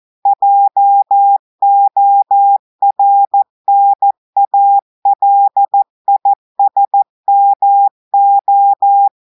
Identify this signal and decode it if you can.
Morse code, heard as a keyed tone, about 14 wpm, 800 Hz. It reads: JORNALISMO
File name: audios / portugais / portugais texte en clair 1 vitesse 14 mots minutes